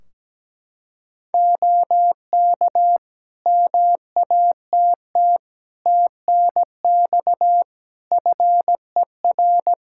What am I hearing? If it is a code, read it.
Morse code, heard as a keyed tone, about 17 wpm, 700 Hz. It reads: OK MATT TNX FER
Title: Practice QSO - Exchange 4 - Ending - 17wpm